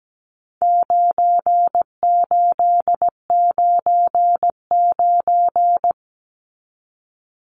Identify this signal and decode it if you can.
Morse code, heard as a keyed tone, about 17 wpm, 700 Hz. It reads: 9899
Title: Numbers - 4-Digit 17wpm